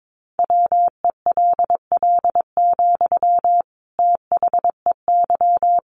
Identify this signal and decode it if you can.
Morse code, heard as a keyed tone, about 22 wpm, 700 Hz. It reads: WELL, THEY
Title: Sentences from Top 1500 Words 22wpm